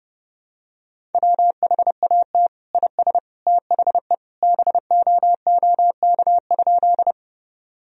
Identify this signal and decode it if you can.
Morse code, heard as a keyed tone, about 30 wpm, 700 Hz. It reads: WHAT IS THE BOOK?
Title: Sentences from Top 1000 Words - Review Part 1 - 30wpm